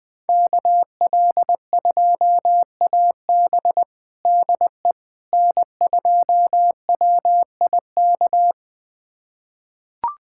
Morse code, heard as a keyed tone, about 20 wpm, 700 Hz. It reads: KL2AB DE N2WIK
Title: USA Callsign DE Callsign - 5 Characters - 20wpm